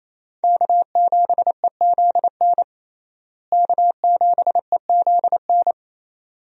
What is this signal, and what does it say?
Morse code, heard as a keyed tone, about 28 wpm, 700 Hz. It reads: K7EZN K7EZN
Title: USA Call Signs - 5 Characters - Repeated 2x - 28wpm